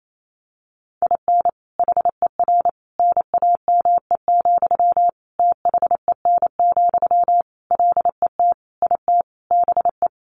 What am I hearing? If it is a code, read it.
Morse code, heard as a keyed tone, about 28 wpm, 700 Hz. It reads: IN HER NAME, THEN, LET IT BE
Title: Sentences from Top 500 Words 28wpm